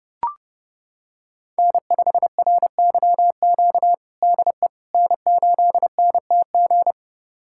Morse code, heard as a keyed tone, about 29 wpm, 700 Hz. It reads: N5RYQ DE N8NTG
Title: USA Callsign DE Callsign - 5 Characters - 30wpm